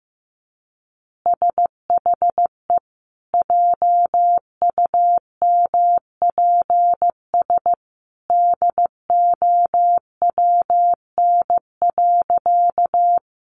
Morse code, heard as a keyed tone, about 15 wpm, 700 Hz. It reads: SHE JUMPS DOWN.
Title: Easy Phrases 15wpm